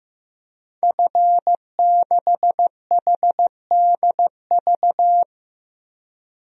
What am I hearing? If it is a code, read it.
Morse code, heard as a keyed tone, about 15 wpm, 700 Hz. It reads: F6HDV
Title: DXCC-100 Call Signs - 5 Characters - 15wpm